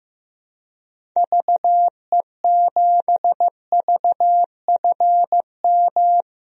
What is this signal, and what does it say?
Morse code, heard as a keyed tone, about 15 wpm, 700 Hz. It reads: VE7VFM
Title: DXCC-100 Call Signs - 6 Characters - Repeated 3x - 15wpm